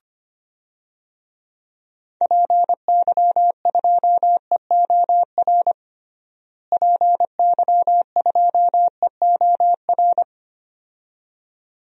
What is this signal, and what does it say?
Morse code, heard as a keyed tone, about 25 wpm, 700 Hz. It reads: PY2EOR PY2EOR
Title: DXCC-100 Call Signs - 6 Characters - Repeated 2x - 25wpm